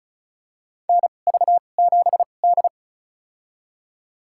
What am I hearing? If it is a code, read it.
Morse code, heard as a keyed tone, about 36 wpm, spaced as for 25 wpm, 700 Hz. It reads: NV7D
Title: USA Call Signs - 4 Characters - 2x Character Spacing - 35wpm